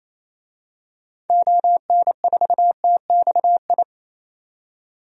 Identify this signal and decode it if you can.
Morse code, heard as a keyed tone, about 28 wpm, 700 Hz. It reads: ON4TXI
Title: DXCC-100 Call Signs - 6 Characters - 28wpm